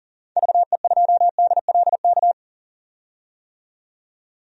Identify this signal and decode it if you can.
Morse code, heard as a keyed tone, about 40 wpm, 700 Hz. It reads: VE2DLK
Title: DXCC-100 Call Signs - 6 Characters - 40wpm